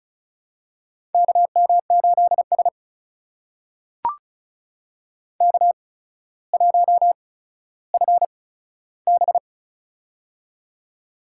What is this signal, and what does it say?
Morse code, heard as a keyed tone, about 36 wpm, spaced as for 15 wpm, 700 Hz. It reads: KM8S K1FB
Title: USA Call Signs - 4 Characters - 8x Character Spacing - 35wpm